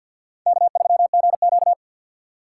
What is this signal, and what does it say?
Morse code, heard as a keyed tone, about 51 wpm, 700 Hz. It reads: K3GQ
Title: USA Call Signs - 3x Character Spacing - 50wpm